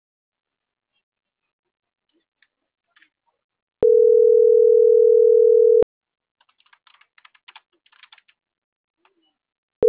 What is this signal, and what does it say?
Ringback tone (North America), 440 + 480 Hz.